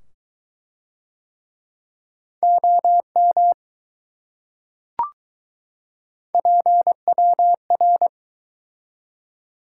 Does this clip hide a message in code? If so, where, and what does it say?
Yes: Morse code from 2.4 to 8.1 s, heard as a keyed tone, about 21 wpm, 700 Hz. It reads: OM PWR